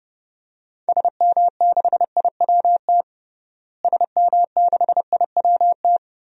Morse code, heard as a keyed tone, about 30 wpm, 700 Hz. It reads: SM6IWT SM6IWT
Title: DXCC-100 Call Signs - 6 Characters - Repeated 2x - 30wpm